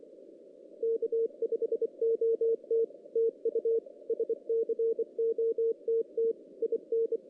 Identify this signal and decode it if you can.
Morse code, heard as a keyed tone, about 24 wpm, 460 Hz. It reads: K5OT TU SCOTT IN
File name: 2016 JAN CW NAQP